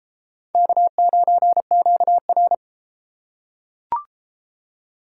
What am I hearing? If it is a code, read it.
Morse code, heard as a keyed tone, about 30 wpm, 700 Hz. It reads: K9QR